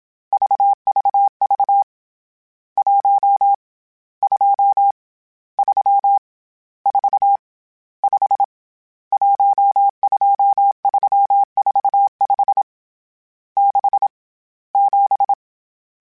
Morse code, heard as a keyed tone, about 27 wpm, 800 Hz. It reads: VVV 1 2 3 4 5 12345 6 7
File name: Misc / Misc-mp3s / Numb @25